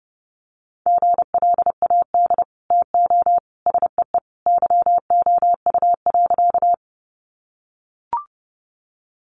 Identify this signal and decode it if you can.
Morse code, heard as a keyed tone, about 30 wpm, 700 Hz. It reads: GLAD TO SEE YOU.
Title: Sentences from Top 800 Words 30wpm